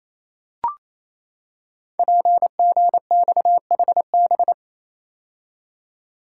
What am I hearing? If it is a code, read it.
Morse code, heard as a keyed tone, about 29 wpm, 700 Hz. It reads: PGXHB
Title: Code Groups - 5 Letters - 28wpm